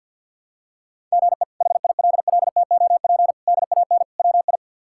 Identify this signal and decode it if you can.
Morse code, heard as a keyed tone, about 49 wpm, 700 Hz. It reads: GE HILLTOP DAN WI
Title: K1USN SST QSO Practice - 50wpm